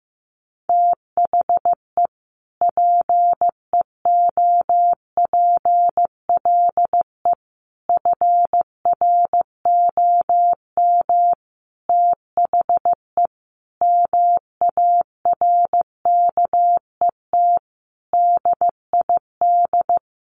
Morse code, heard as a keyed tone, about 14 wpm, 700 Hz. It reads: THE PEOPLE FROM THE MARKET DID
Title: Sentences from Top 600 Words 15wpm